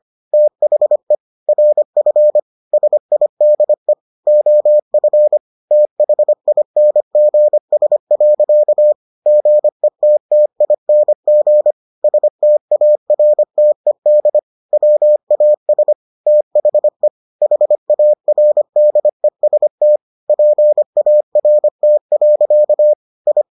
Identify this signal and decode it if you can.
Morse code, heard as a keyed tone, about 25 wpm, 600 Hz. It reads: THE RF SIDE OF THINGS. GETTING STARTED WAS THE HARDEST PART. I